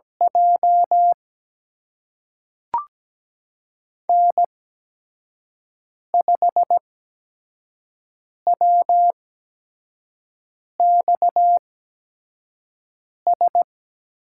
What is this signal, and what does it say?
Morse code, heard as a keyed tone, about 17 wpm, spaced as for 5 wpm, 700 Hz. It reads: J N5WXS
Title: USA Call Signs - 5 Characters - 8x Character Spacing - 17wpm